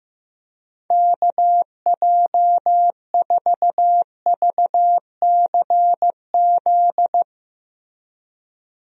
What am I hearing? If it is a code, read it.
Morse code, heard as a keyed tone, about 15 wpm, 700 Hz. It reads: KJ4VCZ
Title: USA Call Signs - 6 Characters - Repeated 2x - 15wpm